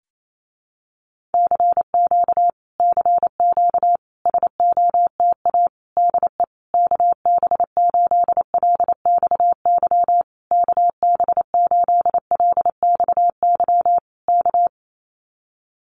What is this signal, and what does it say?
Morse code, heard as a keyed tone, about 28 wpm, 700 Hz. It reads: CQ CQ SOTA DE KB8LXY KB8LXY K
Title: Common QSO Phrases Encore 28wpm